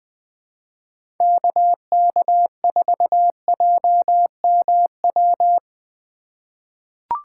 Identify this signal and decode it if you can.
Morse code, heard as a keyed tone, about 20 wpm, 700 Hz. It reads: KK4JMW
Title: USA Call Signs - 6 Characters - 4x Character Spacing - 20wpm